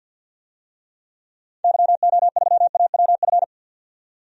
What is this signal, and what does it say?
Morse code, heard as a keyed tone, about 51 wpm, 700 Hz. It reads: YO3AWF
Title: DXCC-100 Call Signs - 6 Characters - 50wpm